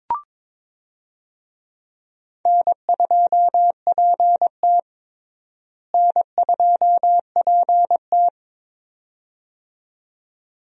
Morse code, heard as a keyed tone, about 22 wpm, 700 Hz. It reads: N2PT N2PT
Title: USA Call Signs - 4 Characters - Repeated 2x - 22wpm